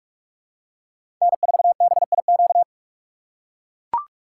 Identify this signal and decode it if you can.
Morse code, heard as a keyed tone, about 43 wpm, 700 Hz. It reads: N4BIQ
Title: USA Call Signs - 5 Characters - 2x Character Spacing - 45wpm